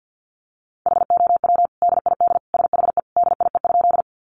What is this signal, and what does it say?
Morse code, heard as a keyed tone, about 50 wpm, 700 Hz. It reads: HOW DID SHE DIE?
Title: Sentences from Top 1300 Words 50wpm